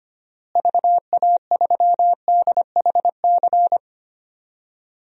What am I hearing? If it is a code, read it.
Morse code, heard as a keyed tone, about 25 wpm, 700 Hz. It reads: VA3DHC
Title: DXCC-100 Call Signs - 6 Characters - Repeated 2x - 25wpm